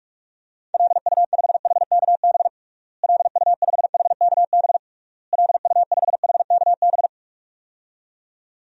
Morse code, heard as a keyed tone, about 46 wpm, 700 Hz. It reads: LU5HKB LU5HKB LU5HKB
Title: DXCC-100 Call Signs - 6 Characters - Repeated 3x - 45wpm